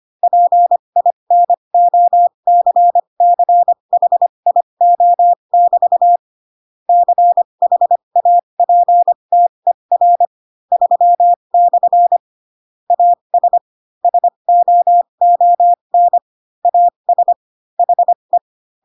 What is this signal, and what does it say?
Morse code, heard as a keyed tone, about 25 wpm, spaced as for 22 wpm, 700 Hz. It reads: PINOCCHIO<BT> CHAPTER 3/ AS SOON AS HE
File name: Pinocchio / Pinocchio ch3 18 25 ik1hge